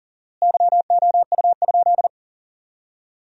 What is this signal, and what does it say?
Morse code, heard as a keyed tone, about 40 wpm, 700 Hz. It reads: YOU?
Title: 4x Word-Spacing - Sentences from Top 500 Words - 40wpm